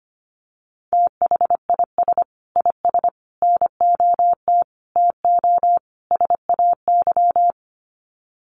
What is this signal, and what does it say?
Morse code, heard as a keyed tone, about 25 wpm, 700 Hz. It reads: THIS IS NOT TO SAY
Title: Sets of 5 Words Encore 25wpm